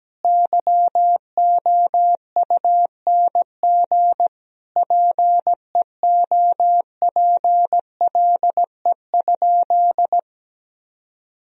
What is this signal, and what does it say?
Morse code, heard as a keyed tone, about 17 wpm, 700 Hz. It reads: YOUNG PEOPLE?
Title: Sentences from Top 1000 Words - Review Part 1 - 17wpm